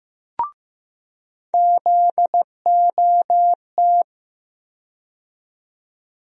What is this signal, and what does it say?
Morse code, heard as a keyed tone, about 15 wpm, 700 Hz. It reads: ZOT
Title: USA Call Sign Postfixes - 3 Letters 15wpm